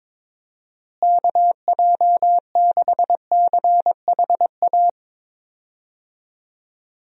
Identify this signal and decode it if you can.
Morse code, heard as a keyed tone, about 22 wpm, 700 Hz. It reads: KJ6CHA
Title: USA Call Signs - 6 Characters - Repeated 2x - 22wpm